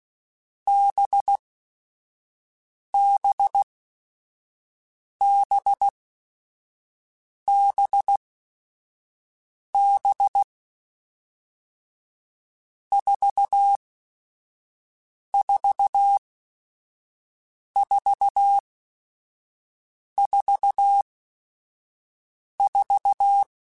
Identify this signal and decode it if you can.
Morse code, heard as a keyed tone, about 16 wpm, spaced as for 5 wpm, 800 Hz. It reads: BBBBB 44444
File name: Groups@5 / Groups@5 mp3s / VB46@5